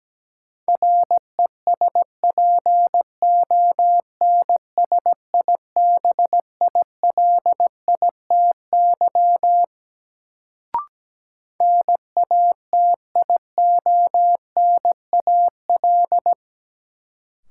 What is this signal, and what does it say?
Morse code, heard as a keyed tone, about 17 wpm, 700 Hz. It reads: RESPONSIBILITY NATIONAL
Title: Sentences from Top 1200 Words - New Word Drill Alt - 17wpm